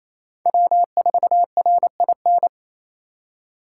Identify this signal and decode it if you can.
Morse code, heard as a keyed tone, about 36 wpm, 700 Hz. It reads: W4RIN